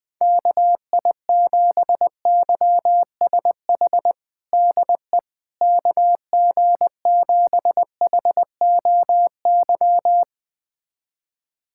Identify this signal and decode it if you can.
Morse code, heard as a keyed tone, about 20 wpm, 700 Hz. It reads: KI7YSH DE KG7HOY
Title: USA Callsign DE Callsign 20wpm